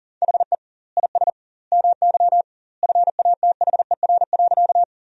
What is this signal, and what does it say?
Morse code, heard as a keyed tone, about 40 wpm, 700 Hz. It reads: HE IS MY FATHER.
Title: 2x Word-Spacing - Sentences from Top 300 Words - 40wpm